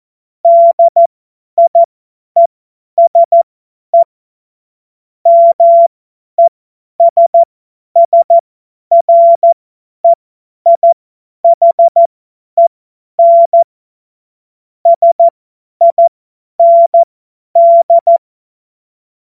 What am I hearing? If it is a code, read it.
Morse code, heard as a keyed tone, about 14 wpm, spaced as for 10 wpm, 680 Hz. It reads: DIESE MESSREIHEN SIND
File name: Morsekurs / Witze / Witz 14. Juni 2022 50 F70